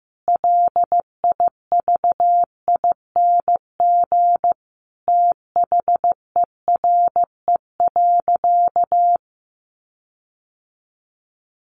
Morse code, heard as a keyed tone, about 15 wpm, 700 Hz. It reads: LIVING THERE.
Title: Sentences from Top 1500 Words 15wpm